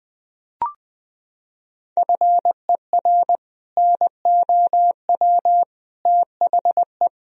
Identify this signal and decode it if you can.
Morse code, heard as a keyed tone, about 20 wpm, 700 Hz. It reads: FER NOW THE
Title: Sentences from Top 200 Words and Abbreviations 20wpm